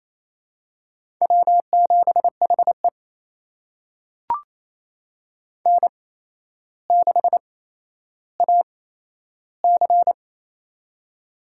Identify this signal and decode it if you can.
Morse code, heard as a keyed tone, about 27 wpm, spaced as for 12 wpm, 700 Hz. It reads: W7HE N6AC
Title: USA Call Signs - 4 Characters - 8x Character Spacing - 28wpm